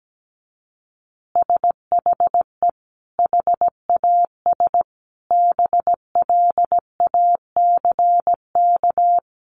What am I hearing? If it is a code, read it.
Morse code, heard as a keyed tone, about 17 wpm, 700 Hz. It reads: SHE HAS BLACK
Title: Easy Phrases 17wpm